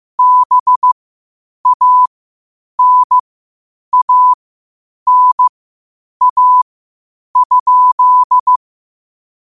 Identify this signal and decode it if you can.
Morse code, heard as a keyed tone, about 15 wpm, spaced as for 8 wpm, 1000 Hz. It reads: BANANA?
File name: qotdfiles / 8wpm